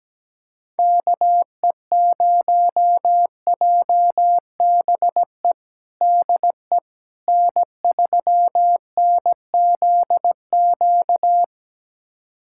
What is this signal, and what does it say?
Morse code, heard as a keyed tone, about 17 wpm, 700 Hz. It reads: KE0JBE DE N3NZQ
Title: USA Callsign DE Callsign 17wpm